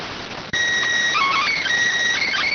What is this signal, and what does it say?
SSTV (slow-scan television), PD 290.